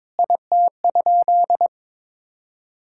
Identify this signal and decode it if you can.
Morse code, heard as a keyed tone, about 22 wpm, 700 Hz. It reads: IT?
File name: practice / 22wpm / Sentences with 4-Word Phrases - 22wpm